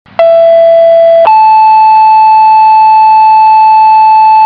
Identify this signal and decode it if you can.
Two-tone page, heard as a one-second tone, then a three-second tone, about 670 then 870 Hz. It is heard from 0.2 to 4.5 s.